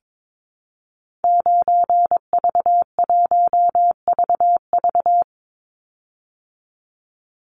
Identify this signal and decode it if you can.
Morse code, heard as a keyed tone, about 22 wpm, 700 Hz. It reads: 9V1VV
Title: Call Signs Repeated 2x - 22wpm